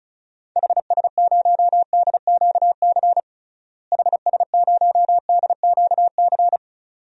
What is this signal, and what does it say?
Morse code, heard as a keyed tone, about 34 wpm, 700 Hz. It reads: HS0DQC HS0DQC
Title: DXCC-100 Call Signs - 6 Characters - Repeated 2x - 35wpm